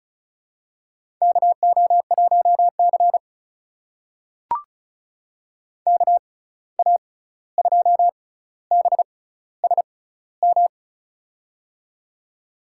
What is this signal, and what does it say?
Morse code, heard as a keyed tone, about 36 wpm, spaced as for 16 wpm, 700 Hz. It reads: KO1C KA2BSM
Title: USA Call Signs - 6x Character Spacing - 35wpm